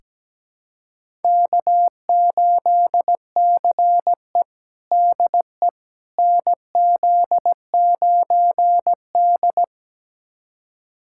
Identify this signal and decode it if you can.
Morse code, heard as a keyed tone, about 17 wpm, 700 Hz. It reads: K8CE DE NZ9D
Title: USA Callsign DE Callsign - 4 Characters - 17wpm